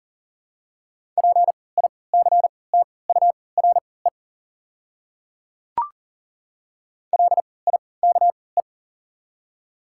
Morse code, heard as a keyed tone, about 40 wpm, spaced as for 23 wpm, 700 Hz. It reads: PICTURE LIKE
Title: 3x Character Spacing - Top 200 Words - 40wpm